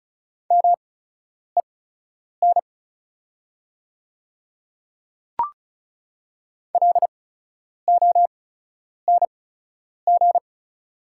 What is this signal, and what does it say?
Morse code, heard as a keyed tone, about 36 wpm, spaced as for 10 wpm, 700 Hz. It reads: MEN LONG
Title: 8x Character Spacing - Top 500 Words - 35wpm